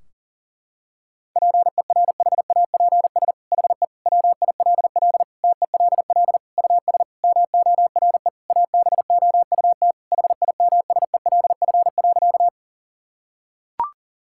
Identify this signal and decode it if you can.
Morse code, heard as a keyed tone, about 40 wpm, 700 Hz. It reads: PERHAPS HE WILL TELL US MORE ABOUT HIMSELF.
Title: Sentences from Top 1600 Words 40wpm